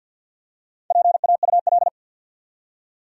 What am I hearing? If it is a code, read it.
Morse code, heard as a keyed tone, about 48 wpm, 700 Hz. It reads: PAUL
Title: USA Common Names - Max 4 Letters - 50wpm